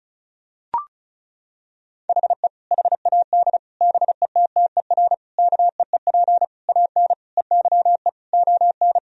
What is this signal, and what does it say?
Morse code, heard as a keyed tone, about 36 wpm, 700 Hz. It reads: HE HAD BETTER KEEP AN EYE ON
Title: Sentences from Top 1000 Words - Review Part 1 - 35wpm